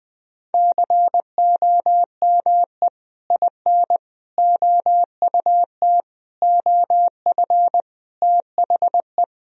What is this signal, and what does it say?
Morse code, heard as a keyed tone, about 20 wpm, 700 Hz. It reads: COME IN OUT OF THE
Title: Sentences from Top 1000 Words - Review Part 1 - 20wpm